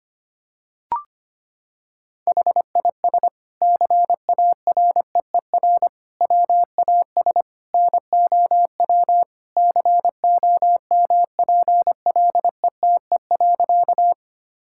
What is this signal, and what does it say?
Morse code, heard as a keyed tone, about 25 wpm, 700 Hz. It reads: HIS CAREER WAS NOW COMPLETE.
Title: Sentences from Top 1000 Words - Review Part 6 - 25wpm